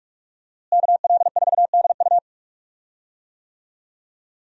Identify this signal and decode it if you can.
Morse code, heard as a keyed tone, about 41 wpm, 700 Hz. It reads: KL4DU
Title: USA Call Signs - 5 Characters - Repeated 3x - 45wpm